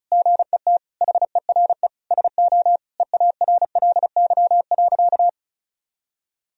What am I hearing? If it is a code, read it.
Morse code, heard as a keyed tone, about 36 wpm, 700 Hz. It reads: GET HERE SO EARLY.
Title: Sentences from Top 1000 Words - Review Part 2 - 35wpm